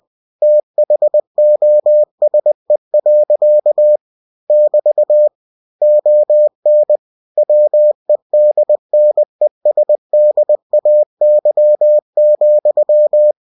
Morse code, heard as a keyed tone, about 20 wpm, 600 Hz. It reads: THOSE. <BT> ON WEDNESDAY,
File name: cw-books / winnie-the-pooh / winnie-20wpm-page-0068